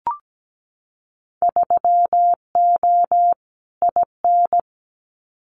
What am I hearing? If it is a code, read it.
Morse code, heard as a keyed tone, about 17 wpm, 700 Hz. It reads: 3O IN
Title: Winter Field Day Exchanges Encore 17wpm